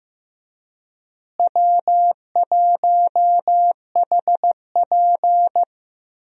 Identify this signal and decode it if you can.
Morse code, heard as a keyed tone, about 15 wpm, 700 Hz. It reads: W1HP
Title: USA Call Signs - 4 Characters - Repeated 3x - 15wpm